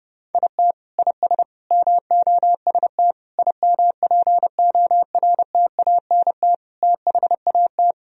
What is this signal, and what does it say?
Morse code, heard as a keyed tone, about 32 wpm, 700 Hz. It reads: IT IS MOST IMPORTANT THAT
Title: Sentences from Top 1000 Words - Review Part 1 - 30wpm